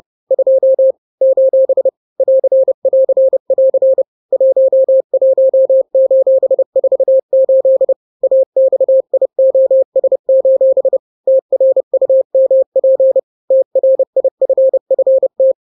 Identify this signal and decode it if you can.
Morse code, heard as a keyed tone, about 30 wpm, 530 Hz. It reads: 2 : <AR><AR><AR> 11:48 AXIOS: TRUMP TRIFFT
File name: daily / 29wpm